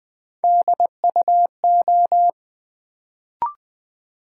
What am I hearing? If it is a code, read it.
Morse code, heard as a keyed tone, about 20 wpm, 700 Hz. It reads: DUO
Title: Build A Word - 1 Letter At A Time - 3-Letter Words - 20wpm